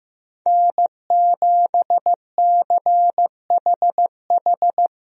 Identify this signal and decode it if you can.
Morse code, heard as a keyed tone, about 15 wpm, 700 Hz. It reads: N7CHH
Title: USA Call Signs - 5 Characters - Repeated 2x - 15wpm